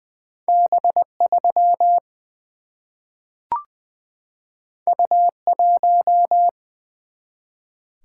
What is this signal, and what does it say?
Morse code, heard as a keyed tone, about 20 wpm, 700 Hz. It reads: B 3 U 1
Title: Single-Letter Single-Number 20wpm